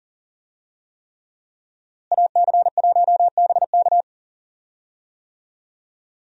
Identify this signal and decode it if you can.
Morse code, heard as a keyed tone, about 40 wpm, 700 Hz. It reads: AC1BK
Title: USA Call Signs - 5 Characters - Repeated 3x - 40wpm